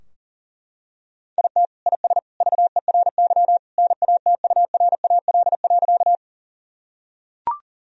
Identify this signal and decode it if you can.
Morse code, heard as a keyed tone, about 37 wpm, 700 Hz. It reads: IT IS VERY NATURAL.
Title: Sentences from Top 1000 Words - Review Part 4 - 40wpm